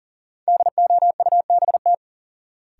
Morse code, heard as a keyed tone, about 40 wpm, 700 Hz. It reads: DOUBT
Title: Sets of 5 Words - New Word Drill - 40wpm